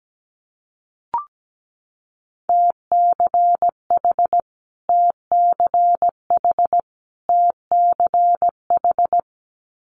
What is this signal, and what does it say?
Morse code, heard as a keyed tone, about 17 wpm, 700 Hz. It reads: TCH TCH TCH
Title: Multi-letter Phonemes Repeated 3x 17wpm